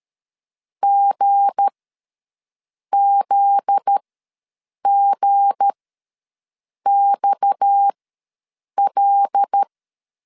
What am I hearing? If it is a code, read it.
Morse code, heard as a keyed tone, about 13 wpm, spaced as for 6 wpm, 800 Hz. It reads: GZGXL
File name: 7 GF9Z ws 12wpm